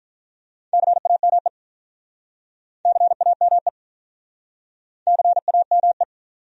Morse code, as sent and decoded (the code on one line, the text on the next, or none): -.-. .- -- . / -.-. .- -- . / -.-. .- -- .
CAME CAME CAME